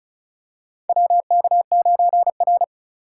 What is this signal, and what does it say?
Morse code, heard as a keyed tone, about 36 wpm, 700 Hz. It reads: WK9R
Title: USA Call Signs - 3x Character Spacing - 35wpm